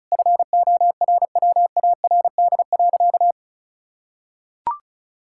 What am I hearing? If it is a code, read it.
Morse code, heard as a keyed tone, about 34 wpm, 700 Hz. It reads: FORWARD.
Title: Pangram Challenge v2 - 35wpm